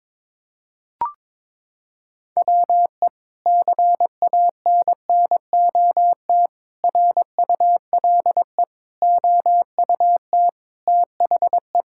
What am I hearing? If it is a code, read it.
Morse code, heard as a keyed tone, about 22 wpm, 700 Hz. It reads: WE CANNOT RULE OUT THE
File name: practice / 22wpm / Sentences with 8-Word Phrases - 22wpm